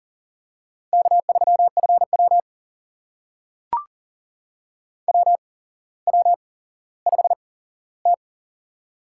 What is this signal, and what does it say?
Morse code, heard as a keyed tone, about 39 wpm, spaced as for 17 wpm, 700 Hz. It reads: K3FW WW5T
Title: USA Call Signs - 4 Characters - 8x Character Spacing - 40wpm